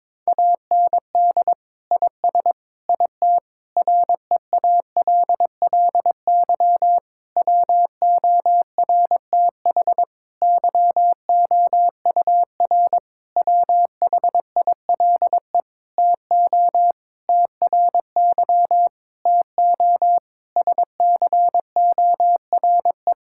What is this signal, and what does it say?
Morse code, heard as a keyed tone, about 22 wpm, 700 Hz. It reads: AND IS IT REALLY WORTH YOUR WHILE TO TRY TO SCORE
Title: Sentences from Top 1600 Words 22wpm